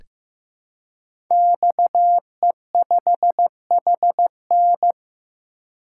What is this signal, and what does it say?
Morse code, heard as a keyed tone, about 15 wpm, 700 Hz. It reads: XE5HN
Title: Code Groups - 5 Letter:Numbers - 15wpm